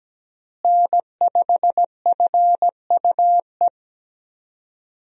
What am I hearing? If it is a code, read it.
Morse code, heard as a keyed tone, about 17 wpm, 700 Hz. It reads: N5FUE
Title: USA Call Signs - 5 Characters - Repeated 2x - 17wpm